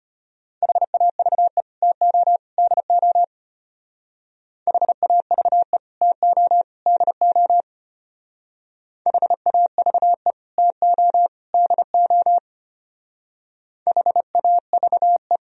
Morse code, heard as a keyed tone, about 32 wpm, 700 Hz. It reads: HAVE TO DO HAVE TO DO HAVE TO DO HAVE